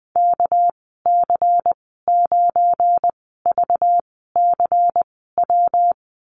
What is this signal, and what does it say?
Morse code, heard as a keyed tone, about 19 wpm, spaced as for 14 wpm, 700 Hz. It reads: KC9VCW
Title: 238 Lesson 38g - Letter q - Call Signs 20wpm